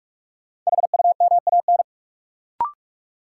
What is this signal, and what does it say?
Morse code, heard as a keyed tone, about 46 wpm, 700 Hz. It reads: HUMAN
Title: Sets of 5 Words - New Word Drill - 45wpm